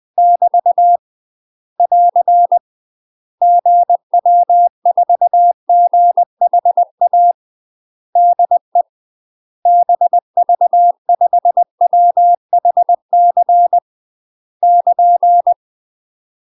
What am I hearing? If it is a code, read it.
Morse code, heard as a keyed tone, about 20 wpm, 700 Hz. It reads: <BT> <AR> GW4GHA DE BV5WHC <KN>
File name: UK MASTER QSOs TRAINING / 20WPM / UK MASTER QSOs 0025